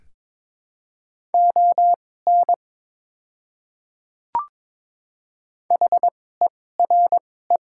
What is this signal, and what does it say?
Morse code, heard as a keyed tone, about 21 wpm, spaced as for 16 wpm, 700 Hz. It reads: ON HERE
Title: 2x Character Spacing - Top 200 Words - 22wpm